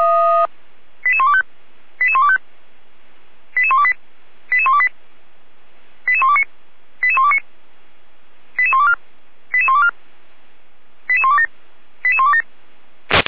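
Five-tone selective call (ZVEI), heard as quick runs of five tones, tones of about 70 ms. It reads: 88125 88125 88128 88128 88129 88129 88124 88124 88127 88127
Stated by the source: ZVEI five-tone call received by SDR, from the Feuerwehr Orlamünde audio server.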